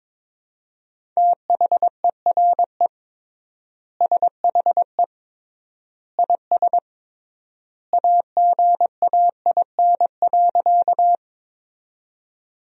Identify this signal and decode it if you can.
Morse code, heard as a keyed tone, about 22 wpm, 700 Hz. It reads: THERE SHE IS AGAIN.
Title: 3x Word-Spacing - Sentences from Top 200 Words - 22wpm